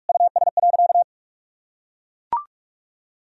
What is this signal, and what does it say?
Morse code, heard as a keyed tone, about 41 wpm, 700 Hz. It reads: US.
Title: Sentences from Top 1000 Words - Review Part 4 - 45wpm